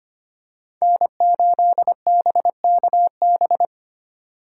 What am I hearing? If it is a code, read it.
Morse code, heard as a keyed tone, about 25 wpm, 700 Hz. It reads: N8BKB
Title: USA Call Signs - 5 Characters - 2x Character Spacing - 25wpm